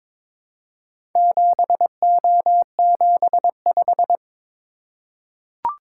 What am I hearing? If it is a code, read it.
Morse code, heard as a keyed tone, about 22 wpm, 700 Hz. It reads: 7O75
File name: practice / 22wpm / Four Letter-Number Combinations - 22wpm